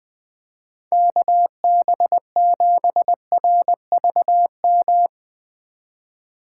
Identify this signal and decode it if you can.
Morse code, heard as a keyed tone, about 20 wpm, 700 Hz. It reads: KB7RVM
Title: Speed Racing - USA 2x3 Call Signs 17wpm